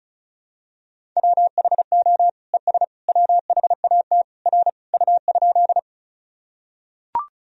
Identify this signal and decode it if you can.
Morse code, heard as a keyed tone, about 36 wpm, 700 Hz. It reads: WHO ES WHAT R U?
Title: Sentences from Top 100 Words and Abbreviations 35wpm